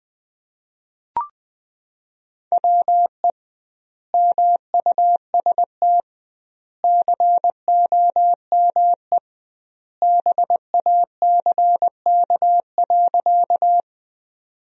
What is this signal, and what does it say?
Morse code, heard as a keyed tone, about 20 wpm, 700 Hz. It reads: WE MUST COME BACK.
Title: 2x Word-Spacing - Sentences from Top 200 Words - 20wpm